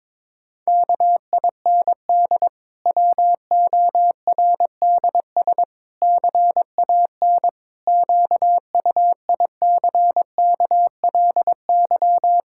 Morse code, heard as a keyed tone, about 22 wpm, 700 Hz. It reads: KIND WORDS CAN QUICKLY
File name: Difficult Sentences with Hints - 22wpm